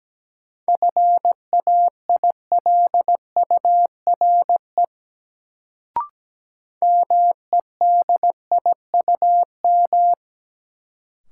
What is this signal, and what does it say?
Morse code, heard as a keyed tone, about 17 wpm, 700 Hz. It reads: FAILURE MEDIUM
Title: Sentences from Top 900 Words - New Word Drill - 17wpm